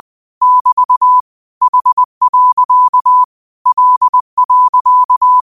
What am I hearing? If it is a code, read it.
Morse code, heard as a keyed tone, about 20 wpm, 1000 Hz. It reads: <BT> H. L.
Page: Quotes for Fri, 15 Aug 2025 in Morse Code at 20 words per minute.